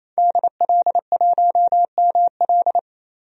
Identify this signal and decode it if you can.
Morse code, heard as a keyed tone, about 28 wpm, 700 Hz. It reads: DL1ML
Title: DXCC-100 Call Signs - 5 Characters - 28wpm